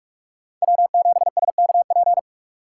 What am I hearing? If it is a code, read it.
Morse code, heard as a keyed tone, about 44 wpm, 700 Hz. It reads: W7SKP